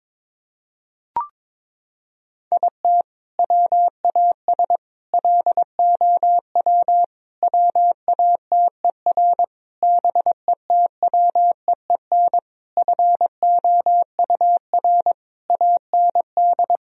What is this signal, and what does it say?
Morse code, heard as a keyed tone, about 22 wpm, 700 Hz. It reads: IT WAS LOW WATER BETWEEN FOUR AND
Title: Sentences from Top 1000 Words - Review Part 2 - 22wpm